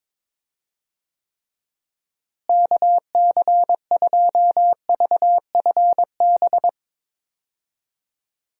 Morse code, heard as a keyed tone, about 22 wpm, 700 Hz. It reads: KC2VFB
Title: USA Call Signs - 6 Characters - Repeated 2x - 22wpm